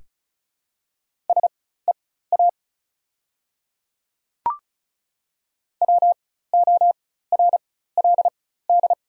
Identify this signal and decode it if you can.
Morse code, heard as a keyed tone, about 36 wpm, spaced as for 16 wpm, 700 Hz. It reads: SEA WORLD
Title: 4x Character Spacing - Top 300 Words - 35wpm